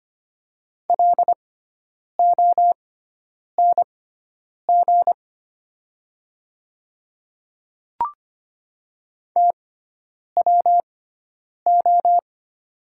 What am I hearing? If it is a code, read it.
Morse code, heard as a keyed tone, about 25 wpm, spaced as for 9 wpm, 700 Hz. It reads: LONG TWO
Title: 6x Character Spacing - Top 100 Words - 25wpm